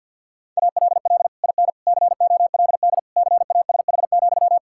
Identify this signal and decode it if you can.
Morse code, heard as a keyed tone, about 50 wpm, 700 Hz. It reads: ALL IN COLD CASH,
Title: Sentences from Top 1000 Words - Review Part 5 - 50wpm